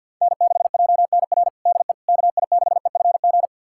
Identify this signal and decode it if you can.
Morse code, heard as a keyed tone, about 49 wpm, 700 Hz. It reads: N6JNR DE KI6EFG
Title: USA Callsign DE Callsign 50wpm